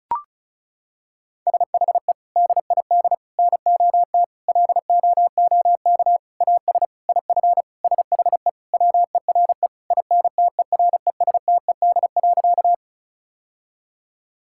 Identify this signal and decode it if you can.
Morse code, heard as a keyed tone, about 34 wpm, 700 Hz. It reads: SHE DID NOT LOOK AS IF SHE WERE INTERESTED.
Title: Sentences from Top 1000 Words - Review Part 6 - 35wpm